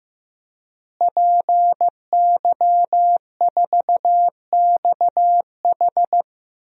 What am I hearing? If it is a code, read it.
Morse code, heard as a keyed tone, about 15 wpm, 700 Hz. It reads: PY4XH
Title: DXCC-100 Call Signs - 5 Characters - Repeated 3x - 15wpm